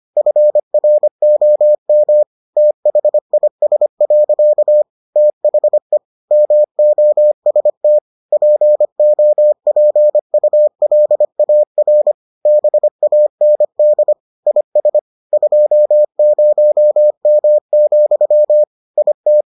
Morse code, heard as a keyed tone, about 25 wpm, 600 Hz. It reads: FROM THIS. THE MOST POPULAR BAND IS 20M, IT